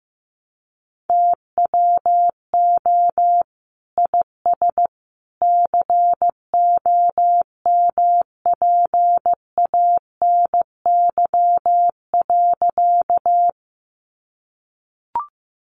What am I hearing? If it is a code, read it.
Morse code, heard as a keyed tone, about 15 wpm, 700 Hz. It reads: TWO IS COMPANY.
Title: Sentences from Top 600 Words 15wpm